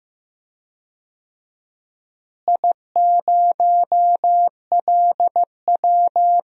Morse code, heard as a keyed tone, about 15 wpm, 700 Hz. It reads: I0LW
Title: DXCC-100 Call Signs - 4 Characters - Repeated 3x - 15wpm